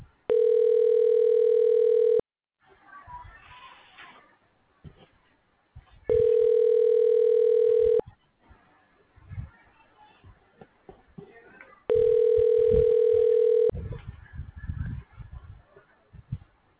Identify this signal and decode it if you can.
Ringback tone (North America), 440 + 480 Hz.